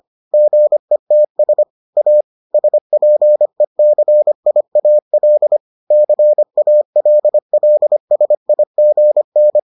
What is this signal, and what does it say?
Morse code, heard as a keyed tone, about 25 wpm, 600 Hz. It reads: GETS A SPECIAL CALLSIGN